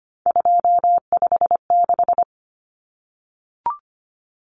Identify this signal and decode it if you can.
Morse code, heard as a keyed tone, about 25 wpm, 700 Hz. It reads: 256
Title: Common Cut Numbers - 3-Digit 25wpm